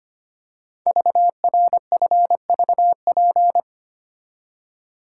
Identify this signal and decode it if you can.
Morse code, heard as a keyed tone, about 25 wpm, 700 Hz. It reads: VRFVP
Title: Code Groups - 5 Letters - 25wpm